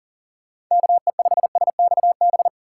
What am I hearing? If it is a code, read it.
Morse code, heard as a keyed tone, about 40 wpm, 700 Hz. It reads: KE5SXB
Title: USA Call Signs - 6 Characters - 6x Character Spacing - 40wpm